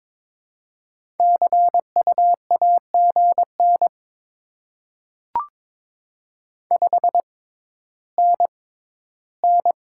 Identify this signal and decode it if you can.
Morse code, heard as a keyed tone, about 21 wpm, spaced as for 12 wpm, 700 Hz. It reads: CU AGN 5NN
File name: Top 20 QSO Elements - 6x Character Spacing - 22wpm